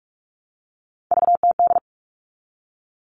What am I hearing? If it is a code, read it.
Morse code, heard as a keyed tone, about 44 wpm, 700 Hz. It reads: VTD
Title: All Cut Numbers - 3-Digit 45wpm